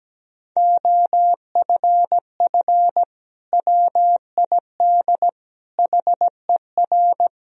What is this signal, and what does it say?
Morse code, heard as a keyed tone, about 17 wpm, 700 Hz. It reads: OFF WID HER
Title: Sentences from Top 300 Words and Abbreviations 17wpm